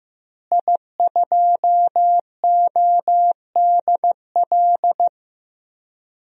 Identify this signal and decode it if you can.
Morse code, heard as a keyed tone, about 15 wpm, 700 Hz. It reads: I2ODL
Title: DXCC-100 Call Signs - 5 Characters - Repeated 3x - 15wpm